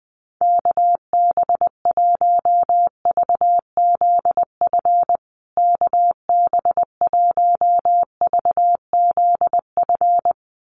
Morse code, heard as a keyed tone, about 20 wpm, 700 Hz. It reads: KB1VZF KB1VZF
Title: USA Call Signs Repeated 2x 20wpm